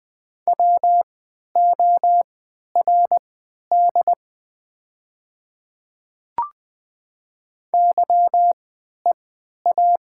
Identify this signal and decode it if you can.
Morse code, heard as a keyed tone, about 20 wpm, spaced as for 11 wpm, 700 Hz. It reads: WORD YEA
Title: 3x Character Spacing - Top 200 Words - 20wpm